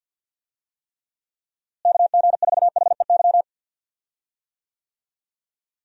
Morse code, heard as a keyed tone, about 51 wpm, 700 Hz. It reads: KG4HEY
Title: USA Call Signs - 6 Characters - Repeated 3x - 50wpm